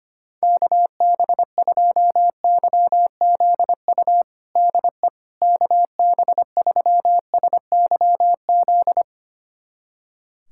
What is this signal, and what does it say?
Morse code, heard as a keyed tone, about 25 wpm, 700 Hz. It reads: KB2YZU DE KB3SYZ
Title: USA Callsign DE Callsign 25wpm